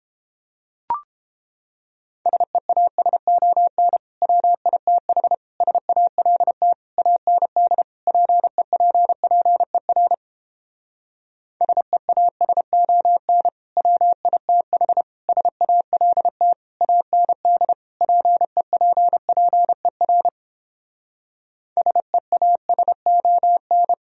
Morse code, heard as a keyed tone, about 31 wpm, 700 Hz. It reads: SEASON WITH SALT AND PEPPER SEASON WITH SALT AND PEPPER SEASON